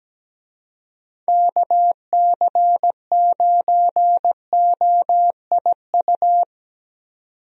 Morse code, heard as a keyed tone, about 17 wpm, 700 Hz. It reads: KC9OIU
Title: USA Call Signs - 6 Characters - 17wpm